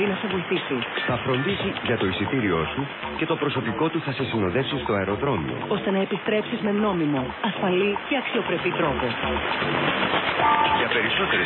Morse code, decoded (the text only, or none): none